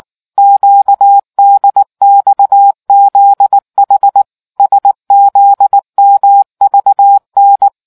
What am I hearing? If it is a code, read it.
Morse code, heard as a keyed tone, about 19 wpm, 800 Hz. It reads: QDXZH SZMVN